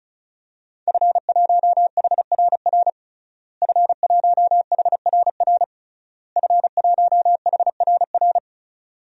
Morse code, heard as a keyed tone, about 36 wpm, 700 Hz. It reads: F1HRR F1HRR F1HRR
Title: DXCC-100 Call Signs - 5 Characters - Repeated 3x - 35wpm